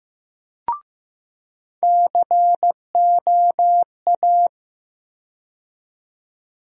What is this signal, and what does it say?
Morse code, heard as a keyed tone, about 15 wpm, 700 Hz. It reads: COA
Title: USA Call Sign Postfixes - 3 Letters 15wpm